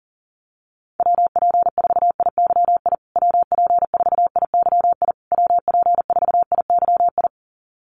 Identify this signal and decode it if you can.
Morse code, heard as a keyed tone, about 40 wpm, 700 Hz. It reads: WP4IYI WP4IYI WP4IYI
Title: USA Call Signs Repeated 3x 40wpm